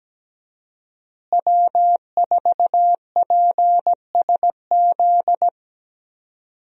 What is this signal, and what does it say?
Morse code, heard as a keyed tone, about 17 wpm, 700 Hz. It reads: W4PSZ
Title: USA Call Signs - 5 Characters - Repeated 3x - 17wpm